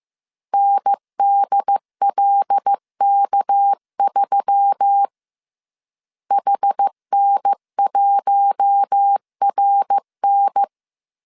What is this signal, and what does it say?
Morse code, heard as a keyed tone, about 15 wpm, 800 Hz. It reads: NDLK3 HN1RN